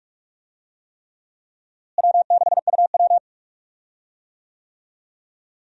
Morse code, heard as a keyed tone, about 46 wpm, 700 Hz. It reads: W6UW
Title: USA Call Signs - 4 Characters - Repeated 3x - 45wpm